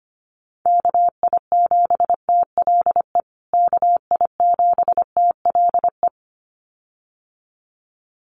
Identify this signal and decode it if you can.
Morse code, heard as a keyed tone, about 25 wpm, 700 Hz. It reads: KI7TLE KI7TLE
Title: USA Call Signs Repeated 2x 25wpm